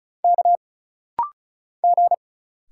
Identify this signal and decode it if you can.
Morse code, heard as a keyed tone, about 37 wpm, 700 Hz. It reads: K G
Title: Single Letters - Rapid-Fire 35wpm